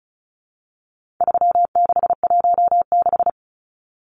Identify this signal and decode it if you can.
Morse code, heard as a keyed tone, about 34 wpm, 700 Hz. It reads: 3616
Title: Numbers - 4-Digit 35wpm